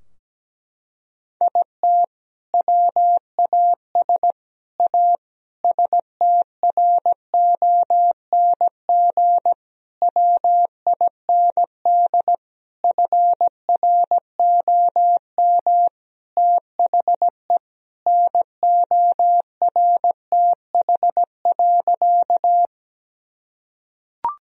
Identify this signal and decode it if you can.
Morse code, heard as a keyed tone, about 17 wpm, 700 Hz. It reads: IT WAS A STRONG WIND FROM THE NORTH.
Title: Sentences from Top 1000 Words - Review Part 2 - 17wpm